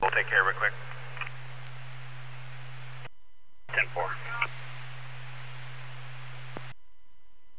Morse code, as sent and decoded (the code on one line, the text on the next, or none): none